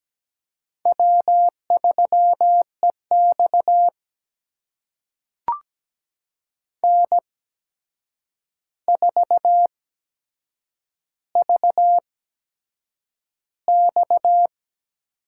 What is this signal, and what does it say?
Morse code, heard as a keyed tone, about 17 wpm, spaced as for 7 wpm, 700 Hz. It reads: W3EX N4VX
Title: USA Call Signs - 4 Characters - 8x Character Spacing - 17wpm